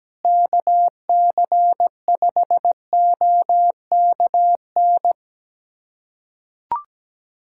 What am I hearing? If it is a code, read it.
Morse code, heard as a keyed tone, about 17 wpm, 700 Hz. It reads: KC5OKN
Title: USA Callsign DE Callsign 17wpm